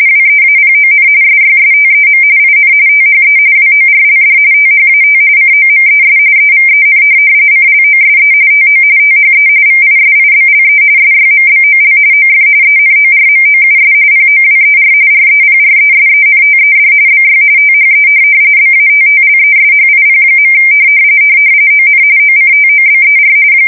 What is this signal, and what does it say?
RTTY, about 45.45 baud. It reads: WEAPON SYSTEM IN THE WORLD THAT COMES EVEN CLOSE TO THE VISIBLE SYMBOL OF ENORMOUS POWER REPRESENTED BY THE BATTLESHIP. ON OCTOBER
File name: fsk170 / USS-New-Jersey